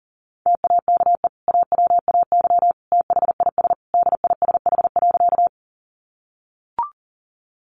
Morse code, heard as a keyed tone, about 40 wpm, 700 Hz. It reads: TAKE AWAY THIS DISH.
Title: Sentences from Top 1300 Words 40wpm